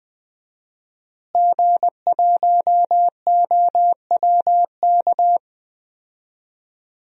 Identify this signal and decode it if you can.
Morse code, heard as a keyed tone, about 20 wpm, 700 Hz. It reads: G1OWK
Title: DXCC-100 Call Signs - 5 Characters - Repeated 3x - 20wpm